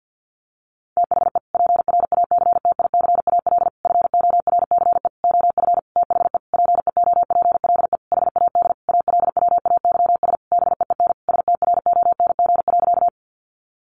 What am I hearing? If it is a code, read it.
Morse code, heard as a keyed tone, about 50 wpm, 700 Hz. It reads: THE PRACTICAL FORCE OF THE PEOPLE HAD ALWAYS BEEN STRONG.
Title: Sentences from Top 1400 Words 50wpm